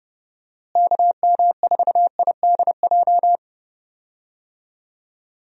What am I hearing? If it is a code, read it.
Morse code, heard as a keyed tone, about 30 wpm, 700 Hz. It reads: KM4IDJ
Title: USA Call Signs - 6 Characters - Repeated 3x - 30wpm